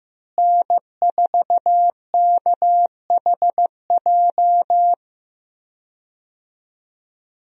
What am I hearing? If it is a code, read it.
Morse code, heard as a keyed tone, about 15 wpm, 700 Hz. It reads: N4KHJ
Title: USA Call Signs - 5 Characters - Repeated 2x - 15wpm